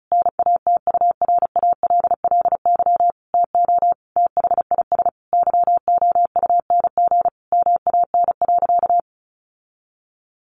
Morse code, heard as a keyed tone, about 34 wpm, 700 Hz. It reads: NATURALLY TO THIS YOUNG MAN.
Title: Sentences from Top 1300 Words 35wpm